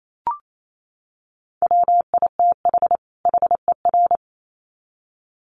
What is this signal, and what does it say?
Morse code, heard as a keyed tone, about 28 wpm, 700 Hz. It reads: WITH HER
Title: Sets of 2 Words 28wpm